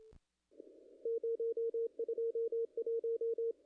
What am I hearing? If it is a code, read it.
Morse code, heard as a keyed tone, about 26 wpm, 440 Hz, with radio noise. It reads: 021
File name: contests / ss-cw-k5tr-op-n5rz -0000